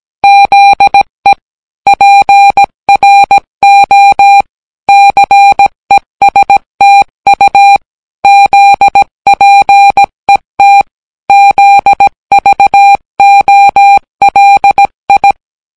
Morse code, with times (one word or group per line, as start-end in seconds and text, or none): 0.2-1.3 ZE
1.9-4.4 PRO
4.9-7.8 CESTU
8.2-10.8 ZPET
11.3-15.3 ZVOLI